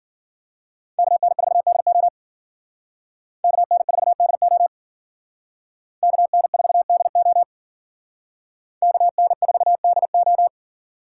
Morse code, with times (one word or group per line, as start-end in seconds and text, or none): 1.0-2.1 KN4DO
3.4-4.7 KN4DO
6.0-7.4 KN4DO
8.8-10.5 KN4DO